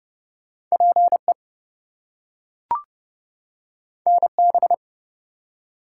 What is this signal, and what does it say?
Morse code, heard as a keyed tone, about 30 wpm, 700 Hz. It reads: PE NB
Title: US States and Canadian Provinces 30wpm